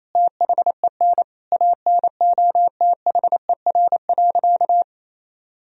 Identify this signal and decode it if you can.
Morse code, heard as a keyed tone, about 27 wpm, 700 Hz. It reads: THEN ANOTHER.
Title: Sentences from Top 1000 Words - Review Part 2 - 28wpm